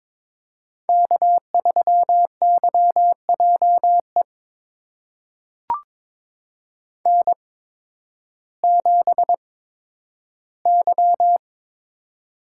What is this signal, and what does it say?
Morse code, heard as a keyed tone, about 22 wpm, spaced as for 12 wpm, 700 Hz. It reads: K3YJE N7Y
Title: USA Call Signs - 5 Characters - 8x Character Spacing - 22wpm